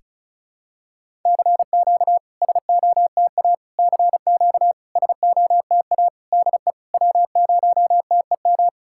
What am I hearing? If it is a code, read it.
Morse code, heard as a keyed tone, about 34 wpm, 690 Hz. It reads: CQ SOTA CQ SOTA DE W0TEM
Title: SOTA QSO Practice 35wpm